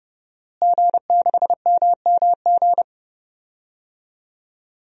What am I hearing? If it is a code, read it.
Morse code, heard as a keyed tone, about 31 wpm, 700 Hz. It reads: G6MMG
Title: DXCC-100 Call Signs - 5 Characters - Repeated 3x - 30wpm